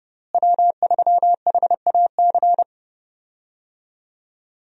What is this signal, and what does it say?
Morse code, heard as a keyed tone, about 31 wpm, 700 Hz. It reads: W3HAC
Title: USA Call Signs - 5 Characters - Repeated 3x - 30wpm